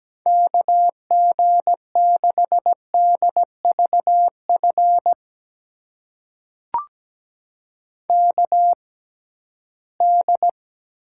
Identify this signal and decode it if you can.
Morse code, heard as a keyed tone, about 17 wpm, spaced as for 13 wpm, 700 Hz. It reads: KG6DVF KD
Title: USA Call Signs - 6 Characters - 6x Character Spacing - 17wpm